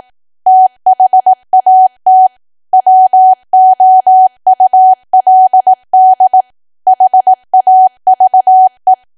Morse code, heard as a keyed tone, about 18 wpm, 750 Hz. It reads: THAT WOULD HAVE
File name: cwa / W1AW-Level2 / 18WPM